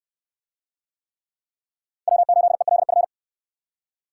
Morse code, heard as a keyed tone, about 65 wpm, 700 Hz. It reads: W8ELF